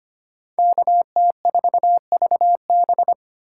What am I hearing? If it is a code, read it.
Morse code, heard as a keyed tone, about 25 wpm, 700 Hz. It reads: KT4VB
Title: USA Call Signs - 5 Characters - 4x Character Spacing - 25wpm